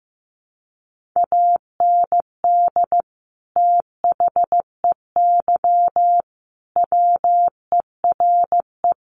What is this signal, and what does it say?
Morse code, heard as a keyed tone, about 15 wpm, 700 Hz. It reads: AND THEY WERE
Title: Sentences from Top 1500 Words 15wpm